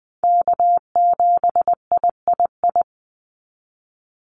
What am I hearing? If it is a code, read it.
Morse code, heard as a keyed tone, about 20 wpm, 700 Hz. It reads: K7III
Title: NA SOTA Chaser Callsigns 20wpm